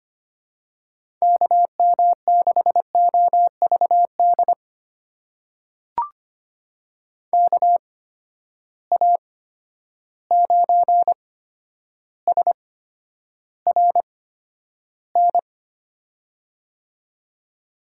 Morse code, heard as a keyed tone, about 25 wpm, spaced as for 11 wpm, 700 Hz. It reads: KM6OVD KA9SRN
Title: USA Call Signs - 6 Characters - 8x Character Spacing - 25wpm